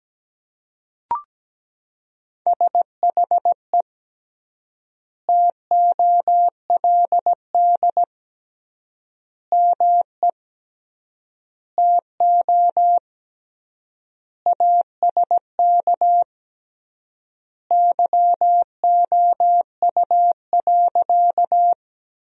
Morse code, heard as a keyed tone, about 17 wpm, 700 Hz. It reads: SHE TOLD ME TO ASK YOU.
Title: 3x Word-Spacing - Sentences from Top 400 Words - 17wpm